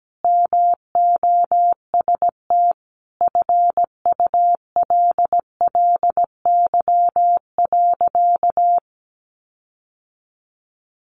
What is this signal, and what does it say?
Morse code, heard as a keyed tone, about 17 wpm, 700 Hz. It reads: MOST FULLY.
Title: Sentences from Top 1300 Words 17wpm